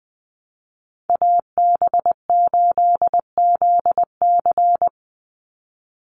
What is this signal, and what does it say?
Morse code, heard as a keyed tone, about 20 wpm, 700 Hz. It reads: AB8ZC
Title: USA Call Signs 20wpm